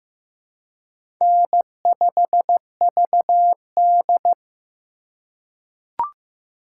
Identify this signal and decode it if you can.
Morse code, heard as a keyed tone, about 15 wpm, 700 Hz. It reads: N5VD
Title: USA Call Signs - 4 Characters - 4x Character Spacing - 15wpm